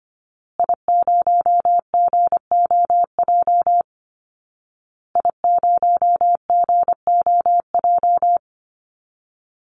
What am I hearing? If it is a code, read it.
Morse code, heard as a keyed tone, about 25 wpm, 700 Hz. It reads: I0GOJ I0GOJ
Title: Call Signs Repeated 2x - 25wpm